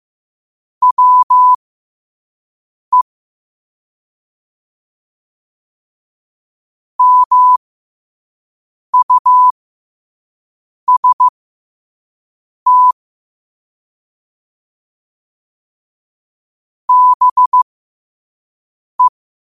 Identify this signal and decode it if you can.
Morse code, heard as a keyed tone, about 15 wpm, spaced as for 5 wpm, 1000 Hz. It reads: WE MUST BE
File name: qotdfiles / 5wpm